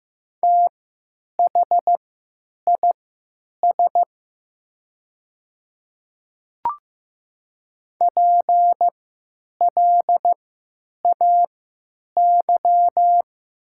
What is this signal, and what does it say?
Morse code, heard as a keyed tone, about 15 wpm, spaced as for 9 wpm, 700 Hz. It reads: THIS PLAY
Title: 3x Character Spacing - Top 200 Words - 15wpm